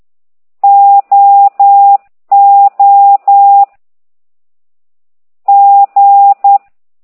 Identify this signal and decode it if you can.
Morse code, heard as a keyed tone, about 10 wpm, 800 Hz. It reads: O O G